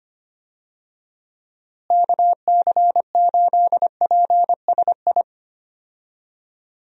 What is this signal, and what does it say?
Morse code, heard as a keyed tone, about 25 wpm, 700 Hz. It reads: KC8PSI
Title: USA Call Signs - 6 Characters - Repeated 3x - 25wpm